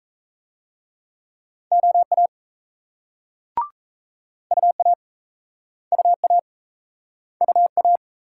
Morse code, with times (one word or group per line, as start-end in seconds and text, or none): 1.7-2.3 OA
4.5-4.9 UA
5.9-6.4 UA
7.4-8.0 UA